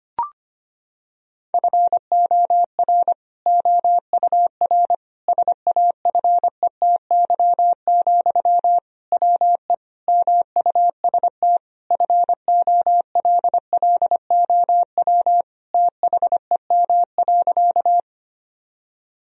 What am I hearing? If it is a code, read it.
Morse code, heard as a keyed tone, about 25 wpm, 700 Hz. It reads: FOR OUR SAFETY, WE MUST FOLLOW THEM.
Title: Sentences from Top 1000 Words - Review Part 4 - 25wpm